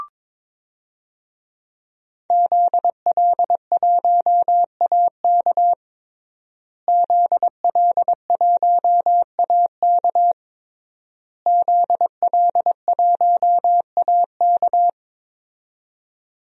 Morse code, heard as a keyed tone, about 21 wpm, 700 Hz. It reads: ZL1AK ZL1AK ZL1AK
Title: DXCC-100 Call Signs - 5 Characters - Repeated 3x - 22wpm